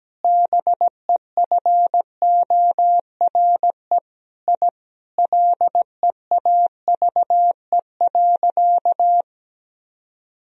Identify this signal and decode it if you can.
Morse code, heard as a keyed tone, about 17 wpm, 700 Hz. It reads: BEFORE I LEAVE.
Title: Sentences from Top 1000 Words - Review Part 1 - 17wpm